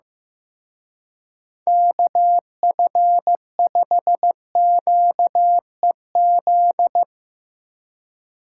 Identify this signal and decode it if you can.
Morse code, heard as a keyed tone, about 16 wpm, 700 Hz. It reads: KF5QEZ
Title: Speed Racing - USA 2x3 Call Signs 15wpm